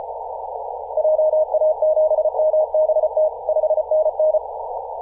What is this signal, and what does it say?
Morse code, heard as a keyed tone, about 34 wpm, 620 Hz, with noise. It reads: JA7MBT 5NN
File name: posts / 2017-aug-07-0652z-ri0li-14-036mhz-cw